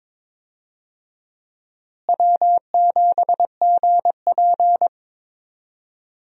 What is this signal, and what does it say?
Morse code, heard as a keyed tone, about 22 wpm, 700 Hz. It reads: W7GP
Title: USA Call Signs - 4 Characters - Repeated 3x - 22wpm